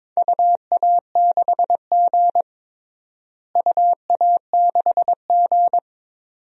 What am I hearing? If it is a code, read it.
Morse code, heard as a keyed tone, about 22 wpm, 700 Hz. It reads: UA6G UA6G
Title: DXCC-100 Call Signs - 4 Characters - Repeated 2x - 22wpm